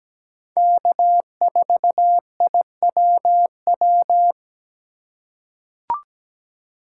Morse code, heard as a keyed tone, about 17 wpm, 700 Hz. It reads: K4IWW
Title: USA Call Signs - 5 Characters - 3x Character Spacing - 17wpm